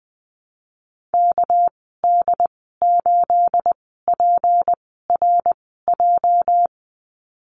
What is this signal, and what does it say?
Morse code, heard as a keyed tone, about 20 wpm, spaced as for 14 wpm, 700 Hz. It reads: KD8PRJ
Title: 238 Lesson 38g - Letter q - Call Signs 20wpm